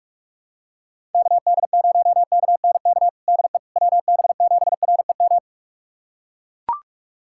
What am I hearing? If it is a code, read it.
Morse code, heard as a keyed tone, about 43 wpm, 700 Hz. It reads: KD0KNK DE WB7REM
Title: USA Callsign DE Callsign - 6 Characters - 45wpm